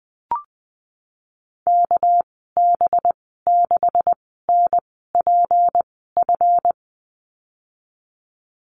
Morse code, heard as a keyed tone, about 20 wpm, spaced as for 14 wpm, 700 Hz. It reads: KB6NPF
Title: 245 Lesson 39g - Letter z - Call Signs 20wpm